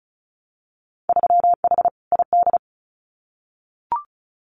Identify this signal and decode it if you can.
Morse code, heard as a keyed tone, about 36 wpm, 700 Hz. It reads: 3H ID
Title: Winter Field Day Exchanges Encore 35wpm